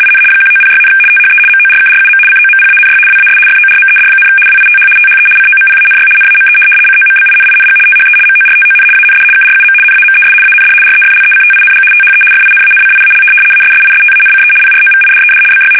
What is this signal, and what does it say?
RTTY, about 75 baud. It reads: IN 1906, THE MORKRUM COMPANY WAS FORMED, WITH THE COMPANY NAME COMBINING THE MORTON AND KRUM NAMES AND REFLECTING THE FINANCIAL ASSISTANCE PROVIDED BY
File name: fsk850-75B / teletype-corp